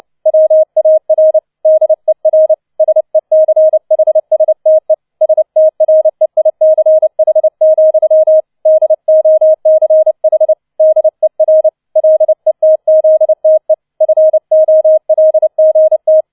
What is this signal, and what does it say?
Morse code, heard as a keyed tone, about 29 wpm, 630 Hz. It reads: WAR DER SECHSTE STREICH, DOCH DER LETZTE FOLGT
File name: 006 Sechster Streich 30wpm